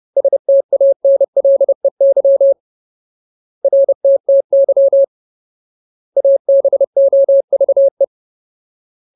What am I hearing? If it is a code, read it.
Morse code, heard as a keyed tone, about 30 wpm, 550 Hz. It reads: STANLEY RTTY ABOVE
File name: Ultimate Word List / 30WPM Ultimate